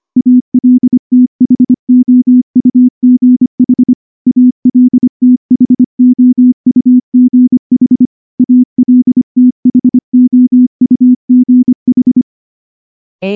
Morse code, heard as a keyed tone, about 24 wpm, 250 Hz. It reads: ALTHOUGH ALTHOUGH ALTHOUGH
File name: morse-training-25WPM